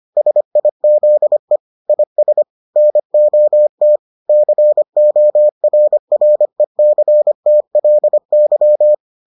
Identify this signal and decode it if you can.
Morse code, heard as a keyed tone, about 25 wpm, 600 Hz. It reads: SIZE IS NOT CORRECTLY